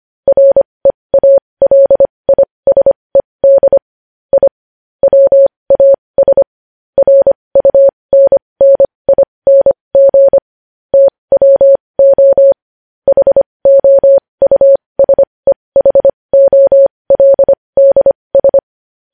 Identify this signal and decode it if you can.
Morse code, heard as a keyed tone, about 25 wpm, spaced as for 20 wpm, 550 Hz. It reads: REALISED I WAS RUNNING TWO HOUSEHOLDS
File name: podcast / News headlines-1x-25x20